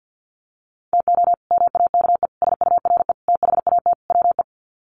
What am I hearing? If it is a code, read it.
Morse code, heard as a keyed tone, about 49 wpm, 700 Hz. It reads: TO MAKE SURE THAT WE
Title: Sets of 5 Words Encore 50wpm